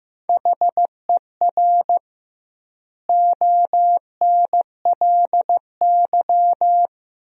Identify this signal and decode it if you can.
Morse code, heard as a keyed tone, about 15 wpm, 700 Hz. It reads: HER ONLY
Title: 2x Word-Spacing - Sentences from Top 400 Words - 15wpm